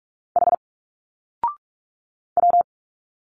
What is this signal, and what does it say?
Morse code, heard as a keyed tone, about 46 wpm, 700 Hz. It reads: H W
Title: Single Letters 45wpm